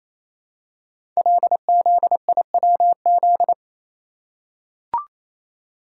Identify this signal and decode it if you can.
Morse code, heard as a keyed tone, about 29 wpm, 700 Hz. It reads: LZIWZ
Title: Code Groups - 5 Letters - 28wpm